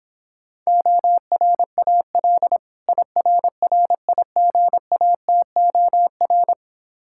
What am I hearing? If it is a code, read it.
Morse code, heard as a keyed tone, about 26 wpm, 700 Hz. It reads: ORAL IRRIGATOR